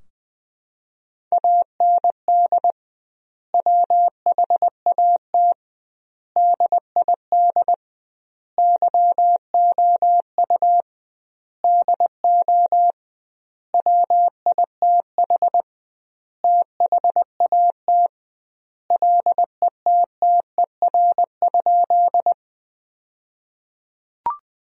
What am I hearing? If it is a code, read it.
Morse code, heard as a keyed tone, about 21 wpm, 700 Hz. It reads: AND WHAT DID YOU DO WITH THAT LETTER?
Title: 2x Word-Spacing - Sentences from Top 300 Words - 20wpm